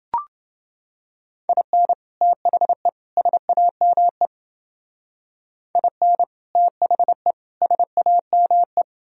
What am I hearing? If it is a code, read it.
Morse code, heard as a keyed tone, about 29 wpm, 700 Hz. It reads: IN THE SAME IN THE SAME